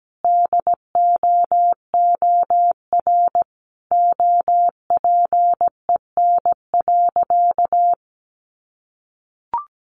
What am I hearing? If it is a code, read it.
Morse code, heard as a keyed tone, about 17 wpm, 700 Hz. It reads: DOOR OPEN.
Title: Sentences from Top 1000 Words 17wpm